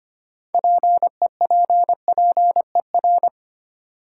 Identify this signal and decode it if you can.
Morse code, heard as a keyed tone, about 25 wpm, 700 Hz. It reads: PEPPER
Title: Sets of 5 Words - New Word Drill - 25wpm